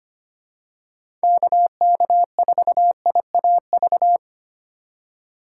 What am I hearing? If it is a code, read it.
Morse code, heard as a keyed tone, about 25 wpm, 700 Hz. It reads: KK4IAV
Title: USA Call Signs - 6 Characters - 2x Character Spacing - 25wpm